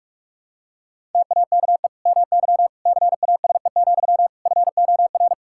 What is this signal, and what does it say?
Morse code, heard as a keyed tone, about 46 wpm, 700 Hz. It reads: TAKE MY CASE, FOR
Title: Sentences from Top 1000 Words - Review Part 5 - 45wpm